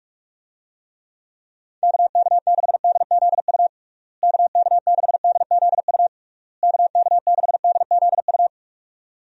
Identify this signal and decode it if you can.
Morse code, heard as a keyed tone, about 44 wpm, 700 Hz. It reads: KK6DZU KK6DZU KK6DZU
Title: USA Call Signs - 6 Characters - Repeated 3x - 45wpm